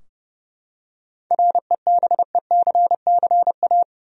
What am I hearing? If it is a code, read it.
Morse code, heard as a keyed tone, about 31 wpm, 700 Hz. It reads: REBECCA
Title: USA Common Names - 30wpm